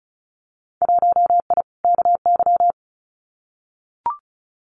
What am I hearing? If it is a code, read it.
Morse code, heard as a keyed tone, about 34 wpm, 700 Hz. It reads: 1I KY
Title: Winter Field Day Exchanges 35wpm